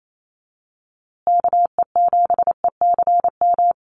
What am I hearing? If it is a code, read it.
Morse code, heard as a keyed tone, about 28 wpm, 700 Hz. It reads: KE7ECM
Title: USA Call Signs Repeated 2x 28wpm